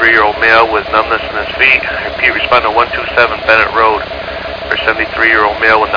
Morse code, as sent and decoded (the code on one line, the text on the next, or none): none